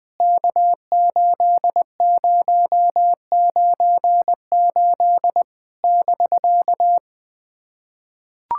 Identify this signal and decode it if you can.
Morse code, heard as a keyed tone, about 20 wpm, 700 Hz. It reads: K8098 <BK>
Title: POTA QSO Practice - 20wpm